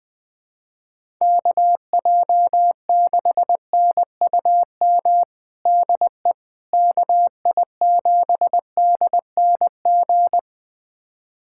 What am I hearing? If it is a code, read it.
Morse code, heard as a keyed tone, about 20 wpm, 700 Hz. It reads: KJ6NUM DE KI7DNG
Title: USA Callsign DE Callsign - 6 Characters - 20wpm